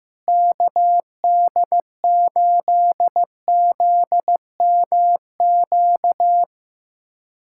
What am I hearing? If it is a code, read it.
Morse code, heard as a keyed tone, about 15 wpm, 700 Hz. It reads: KD8ZMQ
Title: USA Callsign DE Callsign 15wpm